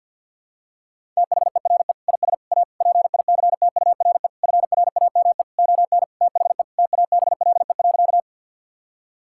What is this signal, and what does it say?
Morse code, heard as a keyed tone, about 51 wpm, 700 Hz. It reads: THERE IS A PICTURE FRAME ON THE TABLE.
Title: Sentences from Top 1000 Words - Review Part 6 - 50wpm